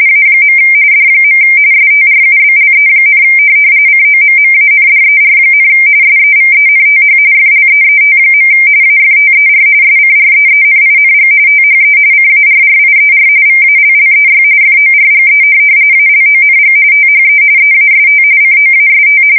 RTTY, about 45.45 baud. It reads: A. D. STRUBLE, COMMANDER, SEVENTH FLEET. AFTER SCREENING THE CARRIER VALLEY FORGE ALONG THE EAST COAST
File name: fsk170 / USS-Missouri